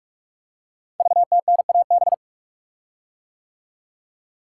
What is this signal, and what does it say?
Morse code, heard as a keyed tone, about 44 wpm, 700 Hz. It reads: VTNAB
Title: Code Groups - 5 Letters - 45wpm